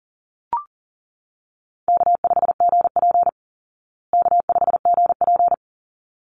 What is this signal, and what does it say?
Morse code, heard as a keyed tone, about 40 wpm, 700 Hz. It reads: K5GP K5GP
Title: Easy Call Signs Repeated 2x - 40wpm